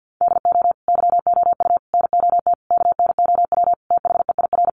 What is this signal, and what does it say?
Morse code, heard as a keyed tone, about 50 wpm, 700 Hz. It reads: DO YOU NOT KNOW THEIR
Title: Sentences from Top 600 Words 50wpm